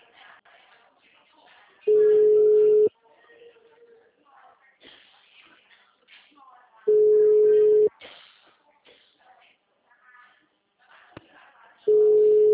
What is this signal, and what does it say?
Ringback tone (Europe, 425 Hz), with speech.
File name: out- -2008- - - .73488